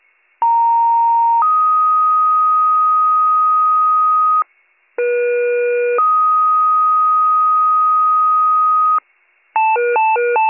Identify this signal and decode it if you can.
Two-tone page, heard as a one-second tone, then a three-second tone, about 930 then 1250 Hz, about 480 then 1250 Hz. It is heard from 0.4 to 9.0 s.